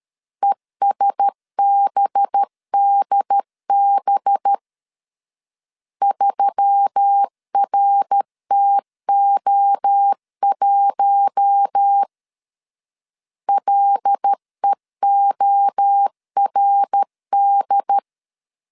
Morse code, heard as a keyed tone, about 13 wpm, 800 Hz. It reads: ESBDB 3RTO1 LEORD
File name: CW / DET / 1 6 1500C S B 12wpm 3431